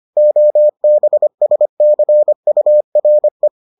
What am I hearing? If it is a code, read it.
Morse code, heard as a keyed tone, about 25 wpm, 600 Hz. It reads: OBSCURE